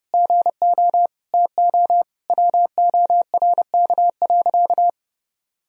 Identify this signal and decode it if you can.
Morse code, heard as a keyed tone, about 30 wpm, 700 Hz. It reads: GO TO WORK.
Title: Sentences from Top 1000 Words - Review Part 2 - 30wpm